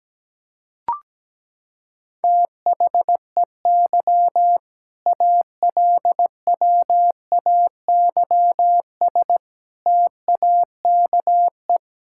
Morse code, heard as a keyed tone, about 17 wpm, 700 Hz. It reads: THEY ALWAYS TAKE
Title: Sentences from Top 1000 Words - Review Part 6 - 17wpm